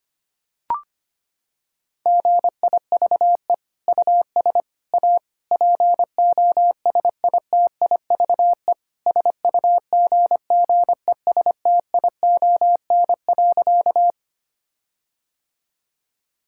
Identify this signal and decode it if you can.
Morse code, heard as a keyed tone, about 25 wpm, 700 Hz. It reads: GIVE US A POSITIVE SUGGESTION.
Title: Sentences from Top 1600 Words 25wpm